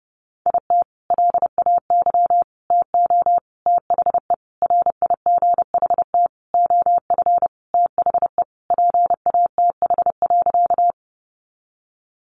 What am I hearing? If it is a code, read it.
Morse code, heard as a keyed tone, about 30 wpm, 700 Hz. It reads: IT LAY TO THE RIGHT OF THE PATH.
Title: Sentences from Top 1400 Words 30wpm